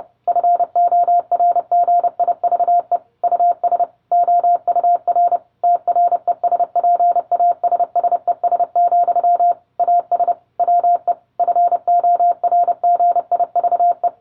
Morse code, heard as a keyed tone, about 30 wpm, 700 Hz. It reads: FORGIVE US OUR TRESPASSES, AS WE FORGIVE